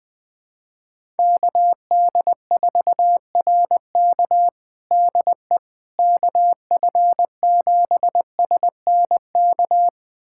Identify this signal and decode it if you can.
Morse code, heard as a keyed tone, about 21 wpm, 700 Hz. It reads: KD4RK DE KF7SNK
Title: USA Callsign DE Callsign 20wpm